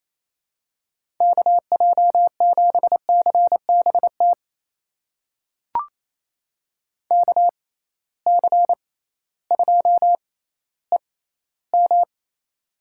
Morse code, heard as a keyed tone, about 28 wpm, spaced as for 15 wpm, 700 Hz. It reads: KJ7CBT KC2EM
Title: USA Call Signs - 6 Characters - 6x Character Spacing - 28wpm